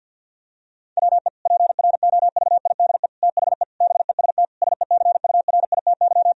Morse code, heard as a keyed tone, about 46 wpm, 700 Hz. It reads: WE PROVIDE THE BEST SECURITY
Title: Sentences from Top 1000 Words - Review Part 4 - 50wpm